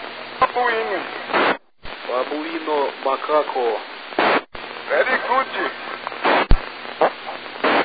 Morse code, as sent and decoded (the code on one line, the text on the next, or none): none